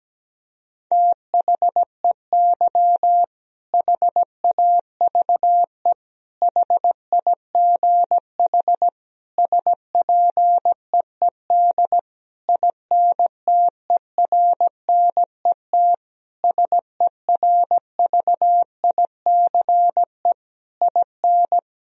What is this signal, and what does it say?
Morse code, heard as a keyed tone, about 17 wpm, 700 Hz. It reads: THEY HAVE HIGH SPEED INTERNET SERVICE IN
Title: Sentences from Top 1000 Words - Review Part 4 - 17wpm